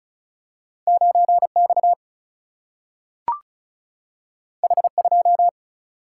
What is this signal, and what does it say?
Morse code, heard as a keyed tone, about 36 wpm, 700 Hz. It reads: 9X H2
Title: Single-Letter Single-Number 35wpm